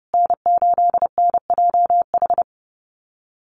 Morse code, heard as a keyed tone, about 30 wpm, 700 Hz. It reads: N8NJH
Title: USA Call Signs Repeated 3x 30wpm